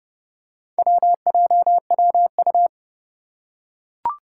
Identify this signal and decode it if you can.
Morse code, heard as a keyed tone, about 31 wpm, 700 Hz. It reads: WJWU
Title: Four Challenging Letter-Number Combinations 30wpm